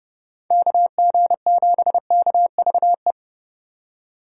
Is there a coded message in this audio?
Yes: Morse code from 0.5 to 3.1 s, heard as a keyed tone, about 32 wpm, 700 Hz. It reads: KG7KVE